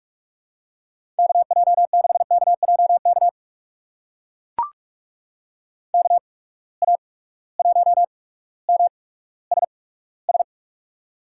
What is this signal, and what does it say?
Morse code, heard as a keyed tone, about 42 wpm, spaced as for 19 wpm, 700 Hz. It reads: KJ6KJK KA1MSS
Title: USA Call Signs - 6 Characters - 8x Character Spacing - 45wpm